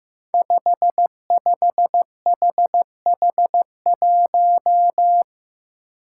Morse code, heard as a keyed tone, about 15 wpm, 700 Hz. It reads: 55HH1
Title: Five Challenging Letter-Number Combinations 15wpm